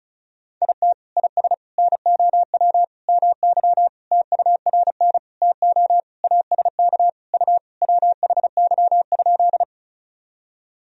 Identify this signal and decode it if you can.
Morse code, heard as a keyed tone, about 34 wpm, 700 Hz. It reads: IT IS NOW MY TURN TO ASK U WHY?
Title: Sentences from Top 200 Words and Abbreviations 35wpm